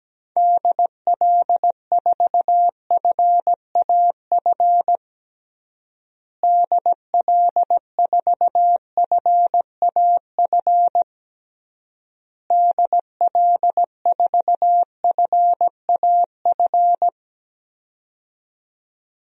Morse code, heard as a keyed tone, about 17 wpm, 700 Hz. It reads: DL4FAF DL4FAF DL4FAF
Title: DXCC-100 Call Signs - 6 Characters - Repeated 3x - 17wpm